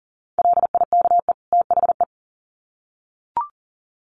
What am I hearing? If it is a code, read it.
Morse code, heard as a keyed tone, about 40 wpm, 700 Hz. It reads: LIKE THE
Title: Sets of 2 Words 40wpm